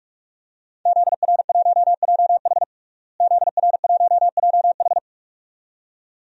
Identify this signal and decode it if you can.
Morse code, heard as a keyed tone, about 43 wpm, 700 Hz. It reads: ZR1JH ZR1JH
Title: DXCC-100 Call Signs - 5 Characters - Repeated 2x - 45wpm